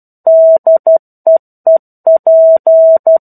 Morse code, heard as a keyed tone, about 12 wpm, 650 Hz. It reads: DEEP
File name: 01-Chapter-01 12 WPM